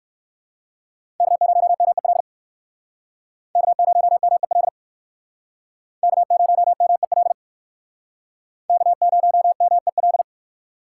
Morse code, heard as a keyed tone, about 57 wpm, 700 Hz. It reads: K0MEL K0MEL K0MEL K0MEL